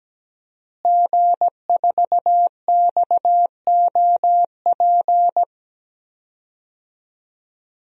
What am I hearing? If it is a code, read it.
Morse code, heard as a keyed tone, about 17 wpm, 700 Hz. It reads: G4XOP
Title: DXCC-100 Call Signs - 5 Characters - Repeated 2x - 17wpm